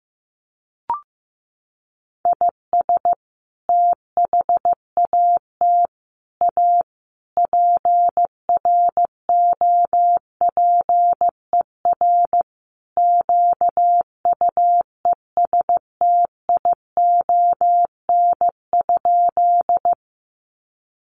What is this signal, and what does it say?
Morse code, heard as a keyed tone, about 15 wpm, 700 Hz. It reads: IS THAT A PROPER QUESTION?
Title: Sentences from Top 1100 Words 15wpm